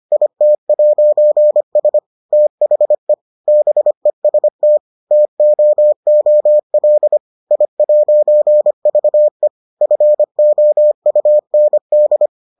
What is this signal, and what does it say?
Morse code, heard as a keyed tone, about 25 wpm, 600 Hz. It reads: IT'S THE BEST TOOL I'VE FOUND